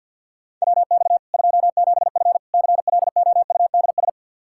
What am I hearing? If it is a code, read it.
Morse code, heard as a keyed tone, about 51 wpm, 700 Hz. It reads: WX 27F CLOUDS
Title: K1USN SST QSO Practice 50wpm